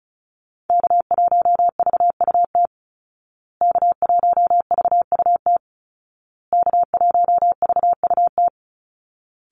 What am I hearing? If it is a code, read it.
Morse code, heard as a keyed tone, about 34 wpm, 700 Hz. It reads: K1VUT K1VUT K1VUT
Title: Call Signs Repeated 2x - 35wpm